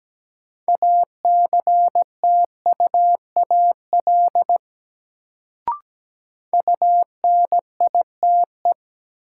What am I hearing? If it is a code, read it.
Morse code, heard as a keyed tone, about 17 wpm, 700 Hz. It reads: ACTUAL UNITE
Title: Sentences from Top 1000 Words - Review Part 5 - New Word Drill - 17wpm